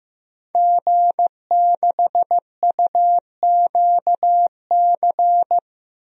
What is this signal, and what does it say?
Morse code, heard as a keyed tone, about 15 wpm, 700 Hz. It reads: G6UQC
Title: DXCC-100 Call Signs - 5 Characters - Repeated 2x - 15wpm